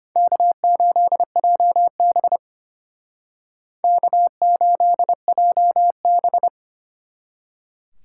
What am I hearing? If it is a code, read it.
Morse code, heard as a keyed tone, about 30 wpm, 700 Hz. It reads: K8JB K8JB
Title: Speed Racing - USA Call Signs 25wpm